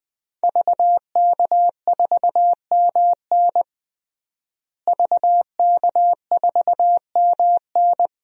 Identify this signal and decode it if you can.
Morse code, heard as a keyed tone, about 20 wpm, 700 Hz. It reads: VK4MN VK4MN
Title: DXCC-100 Call Signs - 5 Characters - Repeated 2x - 20wpm